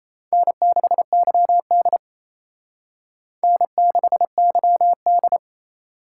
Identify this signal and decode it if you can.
Morse code, heard as a keyed tone, about 35 wpm, 700 Hz. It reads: N6YD N6YD
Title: Speed Racing - USA 1x2 Call Signs 28wpm